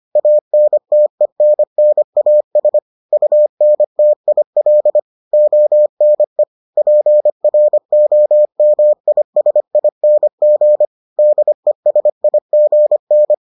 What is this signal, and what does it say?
Morse code, heard as a keyed tone, about 25 wpm, 600 Hz. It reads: ANTENNAS UNTIL ONE PROMISING DESIGN